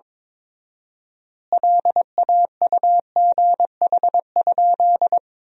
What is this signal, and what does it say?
Morse code, heard as a keyed tone, about 22 wpm, 700 Hz. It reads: LAUGH?
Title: 4x Word-Spacing - Sentences from Top 500 Words - 22wpm